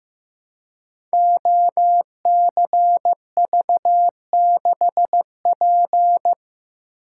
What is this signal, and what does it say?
Morse code, heard as a keyed tone, about 15 wpm, 700 Hz. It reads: OCV6P
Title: Code Groups - 5 Letter:Numbers - 15wpm